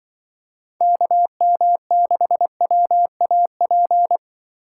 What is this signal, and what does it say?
Morse code, heard as a keyed tone, about 23 wpm, 700 Hz. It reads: KM6WAP
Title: Speed Racing - USA Call Signs 20wpm